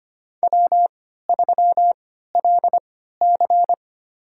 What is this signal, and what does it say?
Morse code, heard as a keyed tone, about 25 wpm, spaced as for 14 wpm, 700 Hz. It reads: W3LC
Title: USA Call Signs - 4 Characters - 3x Character Spacing - 25wpm